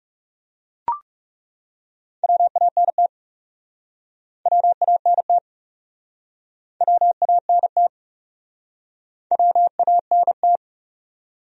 Morse code, heard as a keyed tone, about 37 wpm, 700 Hz. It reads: WANT WANT WANT WANT
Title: Speed-Racing Top 500 Words 30wpm